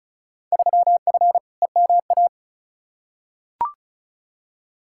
Morse code, heard as a keyed tone, about 36 wpm, 700 Hz. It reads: 3F EMA
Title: Field Day Exchanges Encore - 35wpm